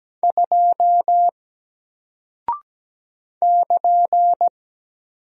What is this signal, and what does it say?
Morse code, heard as a keyed tone, about 17 wpm, 700 Hz. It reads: 2 <KN>
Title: Single Character (Letter:Number:Punctuation:Prosign) Alt 17wpm